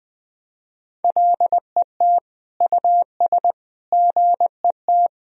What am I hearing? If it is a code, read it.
Morse code, heard as a keyed tone, about 20 wpm, 700 Hz. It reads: LET US GET
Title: Sentences from Top 1000 Words - Review Part 4 - 20wpm